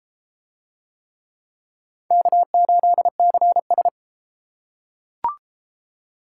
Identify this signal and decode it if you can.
Morse code, heard as a keyed tone, about 33 wpm, 700 Hz. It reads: K8CS